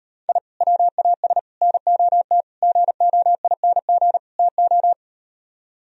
Morse code, heard as a keyed tone, about 38 wpm, 700 Hz. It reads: I WAS NOT GOING TO